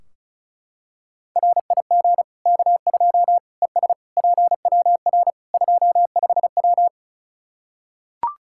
Morse code, heard as a keyed tone, about 36 wpm, 700 Hz. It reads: RIG K2 ES PWR 25W
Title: Practice QSO - Exchange 2 - Info - 35wpm